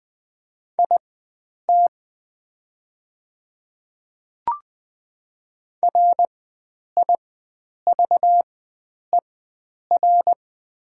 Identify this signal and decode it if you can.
Morse code, heard as a keyed tone, about 20 wpm, spaced as for 9 wpm, 700 Hz. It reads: IT RIVER
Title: 4x Character Spacing - Top 300 Words - 20wpm